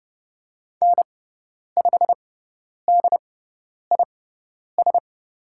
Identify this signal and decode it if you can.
Morse code, heard as a keyed tone, about 30 wpm, spaced as for 10 wpm, 700 Hz. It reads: N5DIS
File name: Lesson 18g - Letter m - Call Signs 30wpm